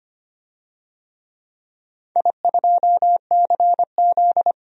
Morse code, heard as a keyed tone, about 25 wpm, 690 Hz. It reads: I2CZ
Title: DXCC-100 Call Signs - 4 Characters - 25wpm